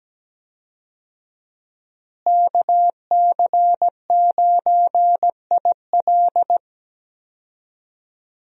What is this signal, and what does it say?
Morse code, heard as a keyed tone, about 17 wpm, 700 Hz. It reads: KC9IL
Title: USA Call Signs - 5 Characters - Repeated 2x - 17wpm